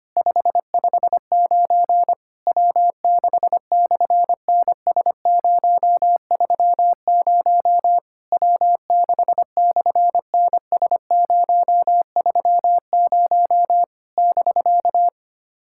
Morse code, heard as a keyed tone, about 24 wpm, 700 Hz. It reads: 559 W6/NS030 W6/NS030 <BK>
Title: SOTA QSO Practice 25wpm